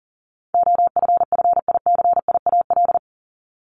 Morse code, heard as a keyed tone, about 40 wpm, 700 Hz. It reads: OFFICIAL
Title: Top 751-1000 Words 40wpm